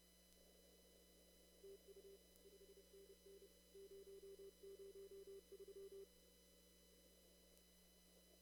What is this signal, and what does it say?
Morse code, heard as a keyed tone, about 30 wpm, 390 Hz. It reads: TU 5NN 003
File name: CW / kt5j wpx cw-op-k5pi